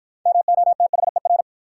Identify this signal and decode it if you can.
Morse code, heard as a keyed tone, about 53 wpm, 700 Hz. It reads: MOTHER